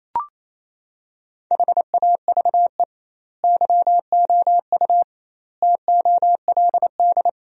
Morse code, heard as a keyed tone, about 28 wpm, 700 Hz. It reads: HAVE YOU TOLD
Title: 2x Word-Spacing - Sentences from Top 500 Words - 28wpm